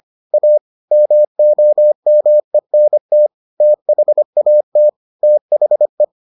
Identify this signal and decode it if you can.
Morse code, heard as a keyed tone, about 25 wpm, 600 Hz. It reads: A MOMENT THAT THE